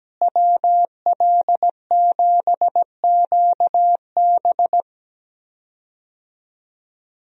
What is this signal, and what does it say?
Morse code, heard as a keyed tone, about 17 wpm, 700 Hz. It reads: WL7QB
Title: USA Call Signs - 5 Characters - Repeated 2x - 17wpm